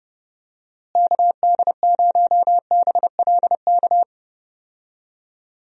Morse code, heard as a keyed tone, about 30 wpm, 700 Hz. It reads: KD0BLK
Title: USA Call Signs - 6 Characters - Repeated 3x - 30wpm